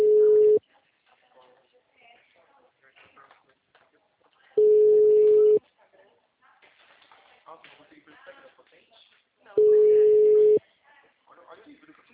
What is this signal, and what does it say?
Ringback tone (Europe, 425 Hz), with speech.